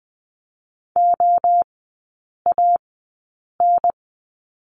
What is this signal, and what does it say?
Morse code, heard as a keyed tone, about 20 wpm, 700 Hz. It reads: O A N
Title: 035 Lesson 7e - Letter s - Three Characters 20wpm